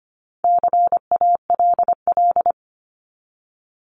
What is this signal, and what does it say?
Morse code, heard as a keyed tone, about 25 wpm, 700 Hz. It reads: CALL
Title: Top 300 Words 25wpm